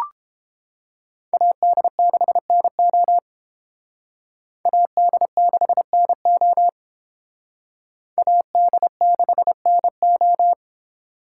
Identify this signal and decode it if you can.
Morse code, heard as a keyed tone, about 28 wpm, 700 Hz. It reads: AD6NO AD6NO AD6NO